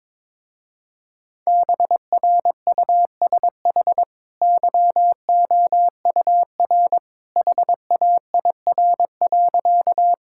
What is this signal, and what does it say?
Morse code, heard as a keyed tone, about 22 wpm, 700 Hz. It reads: BRUSH YOUR HAIR.
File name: practice / 22wpm / Easy Sentences with Hints - 22wpm